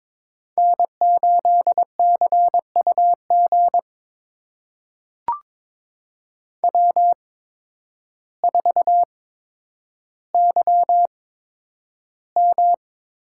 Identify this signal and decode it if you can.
Morse code, heard as a keyed tone, about 22 wpm, spaced as for 10 wpm, 700 Hz. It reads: N8CUG W4YM
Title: USA Call Signs - 5 Characters - 8x Character Spacing - 22wpm